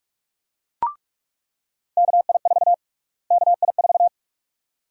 Morse code, heard as a keyed tone, about 47 wpm, 700 Hz. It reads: KI4 KI4
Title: USA Call Sign Prefixes - Repeated 2x - 45wpm